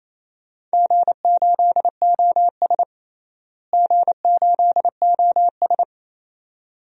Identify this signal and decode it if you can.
Morse code, heard as a keyed tone, about 28 wpm, 700 Hz. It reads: G8OS G8OS
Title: DXCC-100 Call Signs - 4 Characters - Repeated 2x - 28wpm